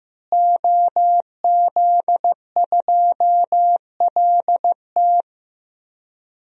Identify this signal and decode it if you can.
Morse code, heard as a keyed tone, about 15 wpm, 700 Hz. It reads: OZ2LT
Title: DXCC-100 Call Signs - 5 Characters - Repeated 2x - 15wpm